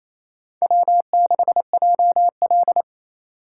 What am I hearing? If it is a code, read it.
Morse code, heard as a keyed tone, about 37 wpm, 700 Hz. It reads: W6JL